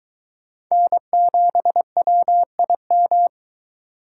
Morse code, heard as a keyed tone, about 23 wpm, 700 Hz. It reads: N7WIM